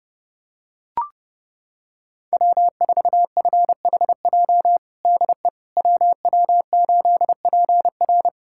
Morse code, heard as a keyed tone, about 30 wpm, 700 Hz. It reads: W4FHJ DE WW8PR
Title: USA Callsign DE Callsign - 5 Characters - 30wpm